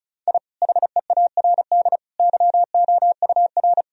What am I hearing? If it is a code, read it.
Morse code, heard as a keyed tone, about 34 wpm, 700 Hz. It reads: I HEARD YOUR
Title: Sentences from Top 1000 Words - Review Part 1 - 35wpm